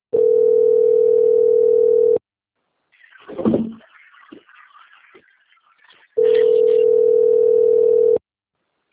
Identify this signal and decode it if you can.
Ringback tone (North America), 440 + 480 Hz.
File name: out- - - - - .8994